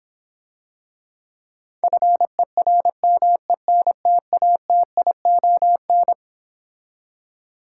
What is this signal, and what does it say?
Morse code, heard as a keyed tone, about 26 wpm, 700 Hz. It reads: FERMENTATION